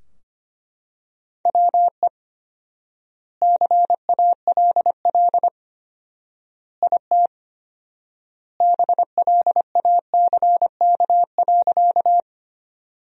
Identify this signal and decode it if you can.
Morse code, heard as a keyed tone, about 25 wpm, 700 Hz. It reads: WE CALL IT BLACK.
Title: 4x Word-Spacing - Sentences from Top 400 Words - 25wpm